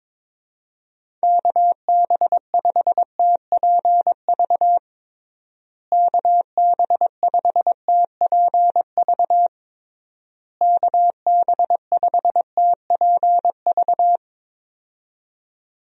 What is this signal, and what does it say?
Morse code, heard as a keyed tone, about 22 wpm, 700 Hz. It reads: KB5TPV KB5TPV KB5TPV
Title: USA Call Signs - 6 Characters - Repeated 3x - 22wpm